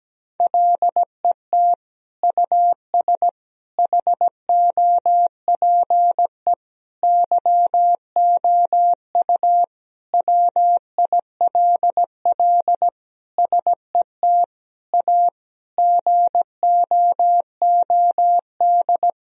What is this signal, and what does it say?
Morse code, heard as a keyed tone, about 17 wpm, 700 Hz. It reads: LET US HOPE YOU WILL SET A GOOD
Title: Sentences from Top 1000 Words - Review Part 1 - 17wpm